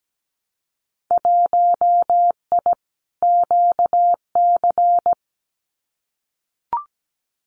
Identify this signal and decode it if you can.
Morse code, heard as a keyed tone, about 17 wpm, 700 Hz. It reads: 1I QC
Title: Winter Field Day Exchanges 17wpm